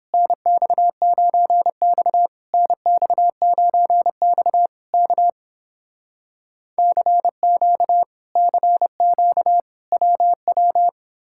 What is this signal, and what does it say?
Morse code, heard as a keyed tone, about 30 wpm, 700 Hz. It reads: NX9X NX9X K CQ CQ WW